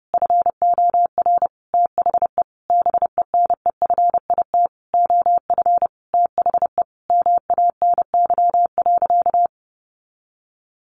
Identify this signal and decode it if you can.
Morse code, heard as a keyed tone, about 30 wpm, 700 Hz. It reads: FOR THE BENEFIT OF THE MANY.
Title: Sentences from Top 1300 Words 30wpm